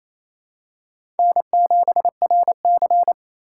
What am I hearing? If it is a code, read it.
Morse code, heard as a keyed tone, about 28 wpm, 700 Hz. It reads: N7RC
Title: USA Call Signs - 4 Characters - 3x Character Spacing - 28wpm